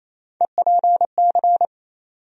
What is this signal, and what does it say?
Morse code, heard as a keyed tone, about 28 wpm, 700 Hz. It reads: EPC
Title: USA Call Sign Postfixes - 3 Letters 28wpm